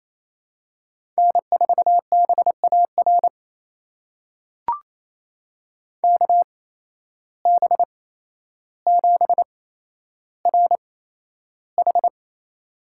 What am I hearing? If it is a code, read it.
Morse code, heard as a keyed tone, about 28 wpm, spaced as for 12 wpm, 700 Hz. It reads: N4BAR KB7RH
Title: USA Call Signs - 8x Character Spacing - 28wpm